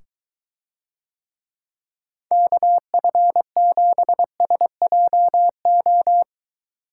Morse code, heard as a keyed tone, about 22 wpm, 700 Hz. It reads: KF7SJO